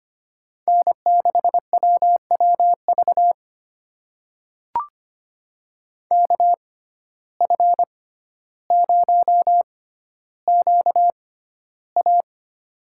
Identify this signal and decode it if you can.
Morse code, heard as a keyed tone, about 25 wpm, spaced as for 13 wpm, 700 Hz. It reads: N6WWV KF0QA
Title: USA Call Signs - 5 Characters - 6x Character Spacing - 25wpm